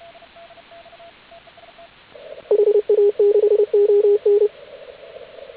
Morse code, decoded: HA6ON

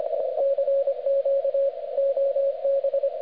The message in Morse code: CQ OB

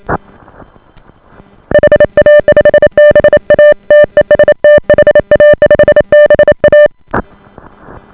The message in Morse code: HA5BA TEST HA5BA

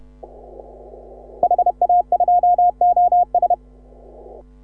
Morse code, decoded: HA2OS

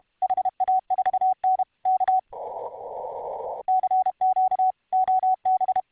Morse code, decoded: HA4N K CQ OB

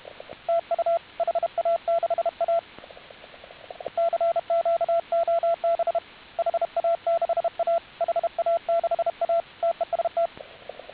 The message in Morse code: TU HA6A CQ OB HA6A HA6A TEST

